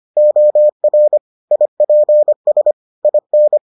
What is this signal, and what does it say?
Morse code, heard as a keyed tone, about 25 wpm, 600 Hz. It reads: OR IPS IN